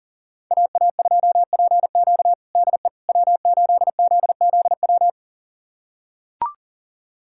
Morse code, heard as a keyed tone, about 37 wpm, 700 Hz. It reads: AA2PQ DE W8ZZW
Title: USA Callsign DE Callsign 40wpm